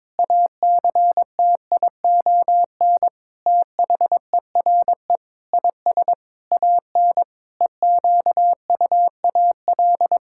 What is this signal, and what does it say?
Morse code, heard as a keyed tone, about 22 wpm, 700 Hz. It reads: ACTION THERE IS AN EQUAL
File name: practice / 22wpm / Sets of 8 Words - 22wpm